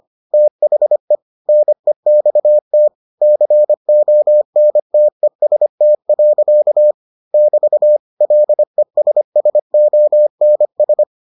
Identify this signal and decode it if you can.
Morse code, heard as a keyed tone, about 25 wpm, 600 Hz. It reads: THE NEXT CONTEST. <BT> LESSONS